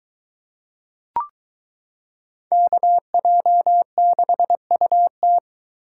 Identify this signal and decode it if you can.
Morse code, heard as a keyed tone, about 23 wpm, 700 Hz. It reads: KJ6UT